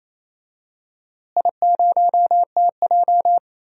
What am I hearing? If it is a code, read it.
Morse code, heard as a keyed tone, about 28 wpm, 700 Hz. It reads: I0TJ
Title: DXCC-100 Call Signs - 4 Characters - Repeated 3x - 28wpm